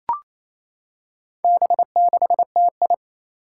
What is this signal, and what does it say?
Morse code, heard as a keyed tone, about 28 wpm, 700 Hz. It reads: B6TI
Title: Four Challenging Letter-Number Combinations 28wpm